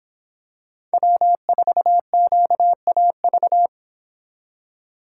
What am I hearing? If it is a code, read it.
Morse code, heard as a keyed tone, about 26 wpm, 700 Hz. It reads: W4QAV